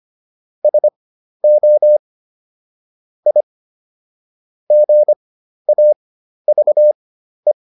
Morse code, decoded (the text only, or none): SO I GAVE